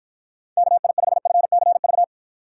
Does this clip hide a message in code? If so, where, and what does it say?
Yes: Morse code from 0.6 to 2.0 s, heard as a keyed tone, about 53 wpm, 700 Hz. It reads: KI5FCV